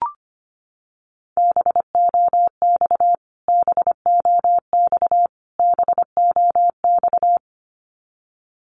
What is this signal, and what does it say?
Morse code, heard as a keyed tone, about 25 wpm, 700 Hz. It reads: BOX BOX BOX
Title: Top 500 Words Repeated 25wpm